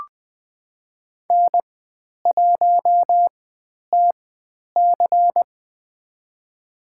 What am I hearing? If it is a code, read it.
Morse code, heard as a keyed tone, about 20 wpm, spaced as for 10 wpm, 700 Hz. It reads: N1TC
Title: 095 Lesson 16g - Letter c - Call Signs 20wpm